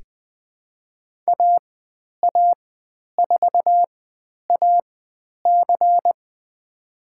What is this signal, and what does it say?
Morse code, heard as a keyed tone, about 20 wpm, spaced as for 10 wpm, 700 Hz. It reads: AA4AC
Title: 095 Lesson 16g - Letter c - Call Signs 20wpm